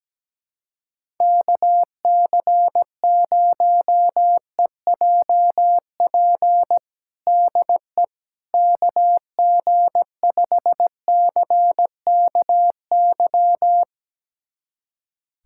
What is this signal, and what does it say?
Morse code, heard as a keyed tone, about 17 wpm, 700 Hz. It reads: KC0EJP DE KG5CKY
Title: USA Callsign DE Callsign - 6 Characters - 17wpm